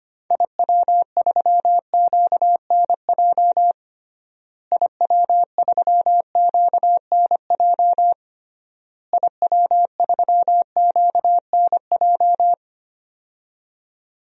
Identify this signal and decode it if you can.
Morse code, heard as a keyed tone, about 25 wpm, 700 Hz. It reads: IW3QNJ IW3QNJ IW3QNJ
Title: DXCC-100 Call Signs - 6 Characters - Repeated 3x - 25wpm